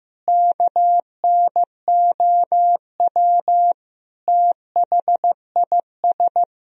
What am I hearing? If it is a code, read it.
Morse code, heard as a keyed tone, about 15 wpm, 700 Hz. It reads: KNOW THIS
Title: Sentences from Top 1000 Words - Review Part 2 - 15wpm